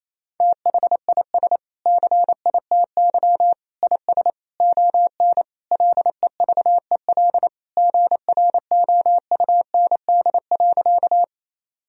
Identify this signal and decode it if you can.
Morse code, heard as a keyed tone, about 27 wpm, 700 Hz. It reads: THIS CITY IS ON LEVEL GROUND.
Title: Sentences from Top 1000 Words - Review Part 4 - 28wpm